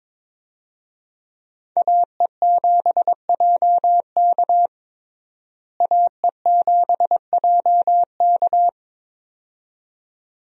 Morse code, heard as a keyed tone, about 22 wpm, 700 Hz. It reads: AE7JK AE7JK
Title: USA Call Signs - 5 Characters - Repeated 2x - 22wpm